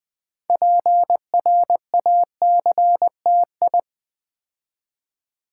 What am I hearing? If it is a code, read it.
Morse code, heard as a keyed tone, about 20 wpm, 700 Hz. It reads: PRACTI
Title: Build A Word - 6 Letters At A Time - 12-Letter Words - 20wpm